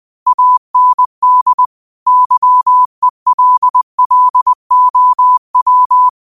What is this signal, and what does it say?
Morse code, heard as a keyed tone, about 20 wpm, 1000 Hz. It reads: AND YELLOW
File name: qotdfiles / 20wpm